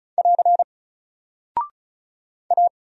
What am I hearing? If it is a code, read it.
Morse code, heard as a keyed tone, about 35 wpm, 700 Hz. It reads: <AR> A
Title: Single Character (Letter:Number:Punctuation:Prosign) 35wpm